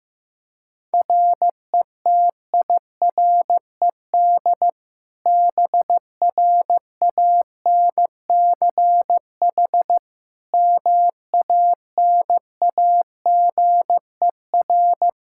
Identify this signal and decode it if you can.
Morse code, heard as a keyed tone, about 15 wpm, 700 Hz. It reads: RETIRED BRANCH MANAGER
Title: Practice QSO - Exchange 3 - More Info - 15wpm